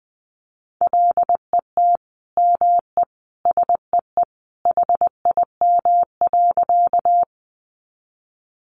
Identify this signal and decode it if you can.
Morse code, heard as a keyed tone, about 20 wpm, 700 Hz. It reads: LET ME SEE HIM.
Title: Sentences from Top 300 Words 20wpm